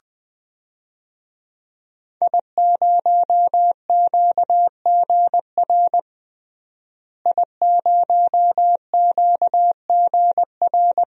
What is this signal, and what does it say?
Morse code, heard as a keyed tone, about 20 wpm, 700 Hz. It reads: I0QGR I0QGR
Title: DXCC-100 Call Signs - 5 Characters - Repeated 2x - 20wpm